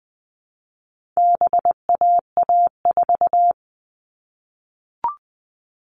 Morse code, heard as a keyed tone, about 20 wpm, 700 Hz. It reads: BAA4
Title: All Cut Numbers - 4-Digit 20wpm